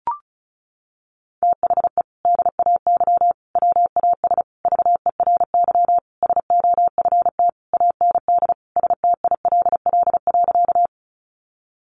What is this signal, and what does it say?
Morse code, heard as a keyed tone, about 34 wpm, 700 Hz. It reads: THE DAY WAS VERY SOFT AND STILL.
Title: Sentences from Top 1500 Words 35wpm